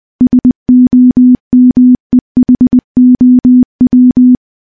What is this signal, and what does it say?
Morse code, heard as a keyed tone, about 20 wpm, 250 Hz. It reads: SOMEHOW
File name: morse-training-20WPM